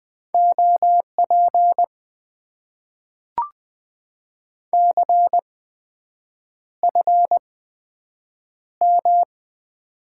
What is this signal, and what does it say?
Morse code, heard as a keyed tone, about 20 wpm, spaced as for 7 wpm, 700 Hz. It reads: OP CFM
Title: Top 90 QSO Elements - 8x Character Spacing - 20wpm